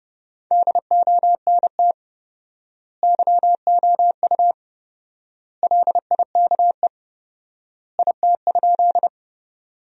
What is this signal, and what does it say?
Morse code, heard as a keyed tone, about 30 wpm, 700 Hz. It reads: DONT YOU LIKE IT?
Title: 4x Word-Spacing - Sentences from Top 300 Words - 30wpm